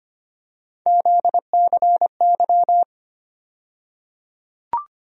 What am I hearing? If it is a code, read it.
Morse code, heard as a keyed tone, about 25 wpm, 700 Hz. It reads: ZCY
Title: USA Call Sign Postfixes - 3 Letters 25wpm